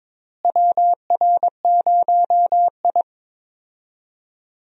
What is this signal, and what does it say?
Morse code, heard as a keyed tone, about 22 wpm, 700 Hz. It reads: WR0I
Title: USA Call Signs - 4 Characters - Repeated 3x - 22wpm